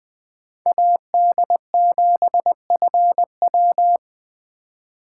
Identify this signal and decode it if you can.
Morse code, heard as a keyed tone, about 24 wpm, 700 Hz. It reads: AD7FW
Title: Speed Racing - USA Call Signs 20wpm